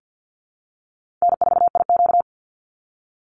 Morse code, heard as a keyed tone, about 50 wpm, 700 Hz. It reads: N4IQ
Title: Easy Call Signs Repeated 3x - 50wpm